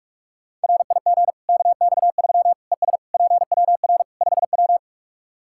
Morse code, heard as a keyed tone, about 46 wpm, 700 Hz. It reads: RIG KX3 ES PWR 5W
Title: Practice QSO - Exchange 2 - Info - 45wpm